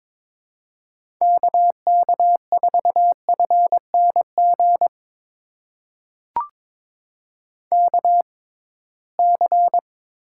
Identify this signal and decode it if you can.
Morse code, heard as a keyed tone, about 21 wpm, spaced as for 16 wpm, 700 Hz. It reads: KK4FNG KC
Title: USA Call Signs - 6 Characters - 6x Character Spacing - 22wpm